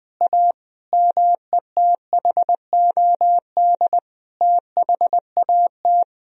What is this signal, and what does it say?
Morse code, heard as a keyed tone, about 20 wpm, 700 Hz. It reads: A METHOD THAT
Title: Sentences from Top 1000 Words - Review Part 3 - 20wpm